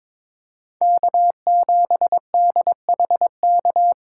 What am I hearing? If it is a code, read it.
Morse code, heard as a keyed tone, about 22 wpm, 700 Hz. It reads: K7DHK
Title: USA Call Signs - 5 Characters - 4x Character Spacing - 22wpm